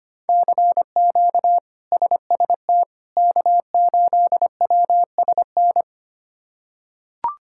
Morse code, heard as a keyed tone, about 25 wpm, 700 Hz. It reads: CQ SST K8WSN
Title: K1USN SST QSO Practice - 25wpm